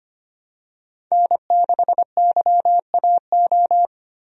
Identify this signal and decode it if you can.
Morse code, heard as a keyed tone, about 25 wpm, 700 Hz. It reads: N6YAO
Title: USA Call Signs - 5 Characters - 25wpm